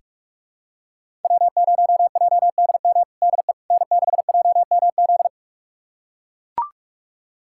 Morse code, heard as a keyed tone, about 41 wpm, 700 Hz. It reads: W0JDM DE N6JMZ
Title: USA Callsign DE Callsign 45wpm